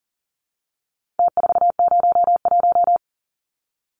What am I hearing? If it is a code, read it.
Morse code, heard as a keyed tone, about 40 wpm, 700 Hz. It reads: T401
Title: Numbers - Contest Style - 4-Digit 40wpm